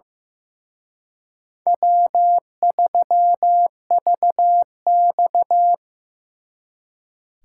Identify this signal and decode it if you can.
Morse code, heard as a keyed tone, about 18 wpm, 700 Hz. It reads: W3VX
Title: Speed Racing - USA Call Signs 15wpm